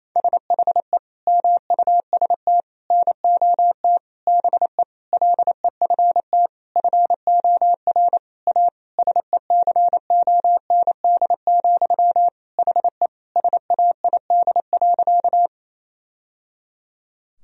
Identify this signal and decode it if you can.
Morse code, heard as a keyed tone, about 28 wpm, 700 Hz. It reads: SHE MUST NOT BE LEFT FOR A SECOND, HE SAID.
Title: Sentences from Top 1000 Words - Review Part 1 - 28wpm